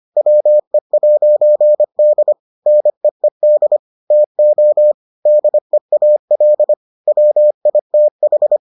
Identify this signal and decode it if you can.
Morse code, heard as a keyed tone, about 25 wpm, 600 Hz. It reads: WE'D NEED TO DEAL WITH